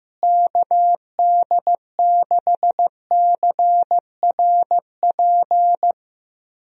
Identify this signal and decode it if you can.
Morse code, heard as a keyed tone, about 16 wpm, 700 Hz. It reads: KD6CRP
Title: USA Call Signs - 6 Characters - Repeated 3x - 15wpm